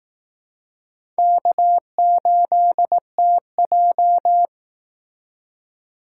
Morse code, heard as a keyed tone, about 18 wpm, 700 Hz. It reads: K8TJ
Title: Speed Racing - USA 1x2 Call Signs 15wpm